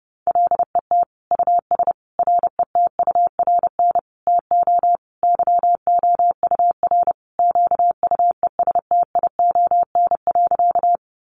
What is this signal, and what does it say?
Morse code, heard as a keyed tone, about 30 wpm, 700 Hz. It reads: LET US RETURN TO YOUR QUESTION.
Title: Sentences from Top 800 Words 30wpm